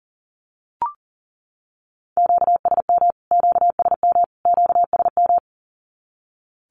Morse code, heard as a keyed tone, about 40 wpm, 700 Hz. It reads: QSM QSM QSM
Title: Q-Codes Repeated 3x 40wpm